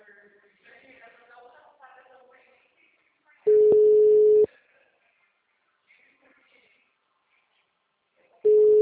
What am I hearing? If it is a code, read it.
Ringback tone (Europe, 425 Hz).
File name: out- -2008- - - .77009